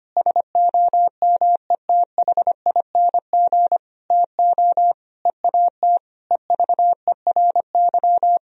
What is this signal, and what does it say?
Morse code, heard as a keyed tone, about 25 wpm, 700 Hz. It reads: SOMETHING TO EAT EVERY
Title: Sentences from Top 1000 Words - Review Part 4 - 25wpm